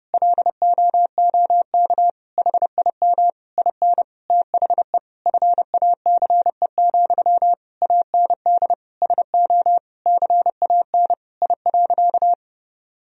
Morse code, heard as a keyed tone, about 30 wpm, 700 Hz. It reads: LOOK HIM IN THE FACE, AND SO CAN I.
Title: Sentences from Top 1000 Words - Review Part 2 - 30wpm